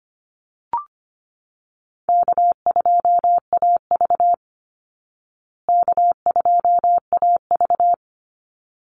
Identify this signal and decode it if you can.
Morse code, heard as a keyed tone, about 25 wpm, 700 Hz. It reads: K2AV K2AV
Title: Easy Call Signs Repeated 2x - 25wpm